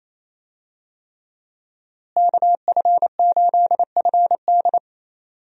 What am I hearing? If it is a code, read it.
Morse code, heard as a keyed tone, about 27 wpm, 700 Hz. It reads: KF8FD
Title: USA Call Signs - 5 Characters - Repeated 3x - 28wpm